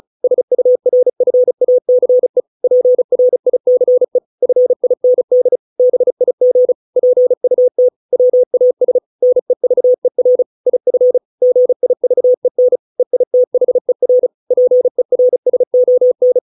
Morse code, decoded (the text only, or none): SURFACE PRICE FIND BIG PUT WAS NEVER IF GIVEN EITHER PERSON